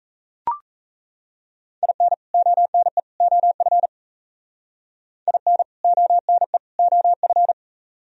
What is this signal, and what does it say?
Morse code, heard as a keyed tone, about 36 wpm, 700 Hz. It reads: IN ONE OF IN ONE OF